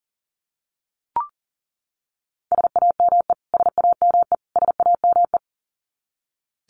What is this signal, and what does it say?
Morse code, heard as a keyed tone, about 37 wpm, 700 Hz. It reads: SAME SAME SAME
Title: Top 500 Words Repeated 40wpm